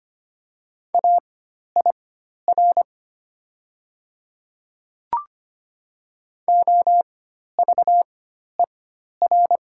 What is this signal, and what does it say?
Morse code, heard as a keyed tone, about 25 wpm, spaced as for 12 wpm, 700 Hz. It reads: AIR OVER
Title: 4x Character Spacing - Top 300 Words - 25wpm